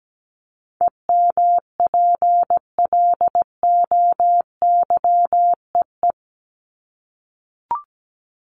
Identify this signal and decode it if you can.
Morse code, heard as a keyed tone, about 17 wpm, 700 Hz. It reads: EMPLOYEE
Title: Top 1501-1750 Words 17wpm